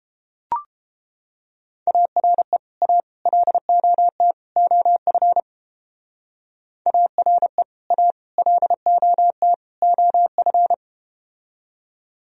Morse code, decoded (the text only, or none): ARE A LOT OF ARE A LOT OF